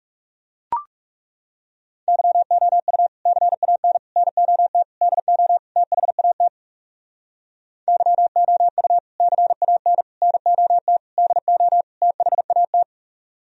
Morse code, heard as a keyed tone, about 37 wpm, 700 Hz. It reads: YOU CAN NOT DO THAT YOU CAN NOT DO THAT
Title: Speed-Racing - Sets of 5 Words 30wpm